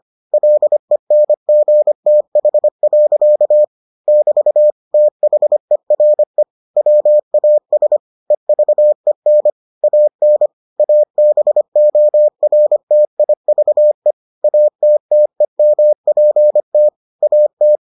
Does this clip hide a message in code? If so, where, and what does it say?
Yes: Morse code from 0.3 to 17.8 s, heard as a keyed tone, about 25 wpm, 600 Hz. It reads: LENGTH. <BT> THERE WAS EVEN AN ABORTIVE ATTEMPT AT